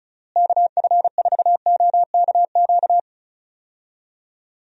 Morse code, heard as a keyed tone, about 34 wpm, 700 Hz. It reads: KF4OKQ
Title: USA Call Signs - 6 Characters - Repeated 3x - 35wpm